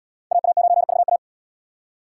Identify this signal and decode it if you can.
Morse code, heard as a keyed tone, about 77 wpm, 700 Hz. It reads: AA9VA